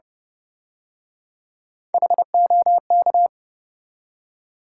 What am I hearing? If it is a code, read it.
Morse code, heard as a keyed tone, about 32 wpm, 700 Hz. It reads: HOK